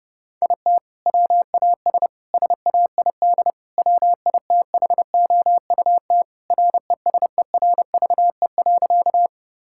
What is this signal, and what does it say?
Morse code, heard as a keyed tone, about 30 wpm, 700 Hz. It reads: IT WAS SAID WITHOUT RESERVE.
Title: Sentences from Top 1000 Words - Review Part 6 - 30wpm